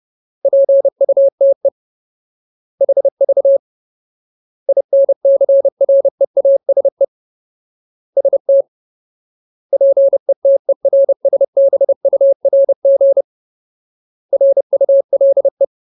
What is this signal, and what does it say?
Morse code, heard as a keyed tone, about 30 wpm, 550 Hz. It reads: PUTE HV INCREASE ST PETERSBURG RULE
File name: Ultimate Word List / 30WPM Ultimate